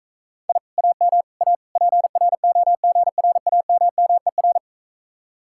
Morse code, heard as a keyed tone, about 45 wpm, 700 Hz. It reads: I AM A PROGRAMMER